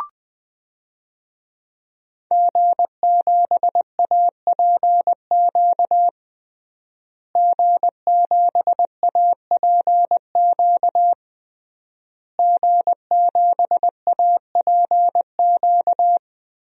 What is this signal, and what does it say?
Morse code, heard as a keyed tone, about 20 wpm, 700 Hz. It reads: G7APQ G7APQ G7APQ
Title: DXCC-100 Call Signs - 5 Characters - Repeated 3x - 20wpm